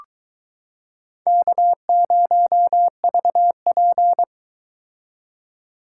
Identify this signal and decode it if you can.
Morse code, heard as a keyed tone, about 22 wpm, 700 Hz. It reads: K0VP